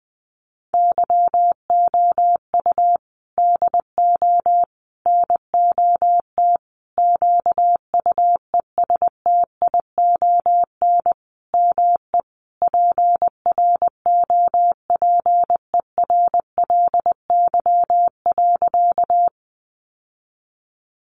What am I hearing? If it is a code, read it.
Morse code, heard as a keyed tone, about 20 wpm, 700 Hz. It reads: YOU DO NOT QUESTION ME PROPERLY.
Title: Sentences from Top 1100 Words 20wpm